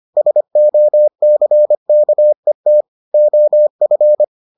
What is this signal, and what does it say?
Morse code, heard as a keyed tone, about 25 wpm, 600 Hz. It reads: SOCKET OF